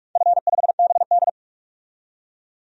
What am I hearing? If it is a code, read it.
Morse code, heard as a keyed tone, about 41 wpm, 700 Hz. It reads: F5BD
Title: DXCC-100 Call Signs - 4 Characters - 45wpm